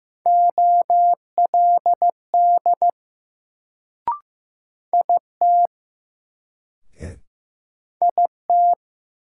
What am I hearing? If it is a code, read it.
Morse code, heard as a keyed tone, about 15 wpm, 700 Hz. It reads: OLD IT IT
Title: Sets of 2 Words - New Word Drill - 15wpm